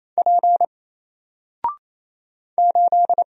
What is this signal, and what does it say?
Morse code, heard as a keyed tone, about 27 wpm, 700 Hz. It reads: P 8
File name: Single-Letter-Number-28wpm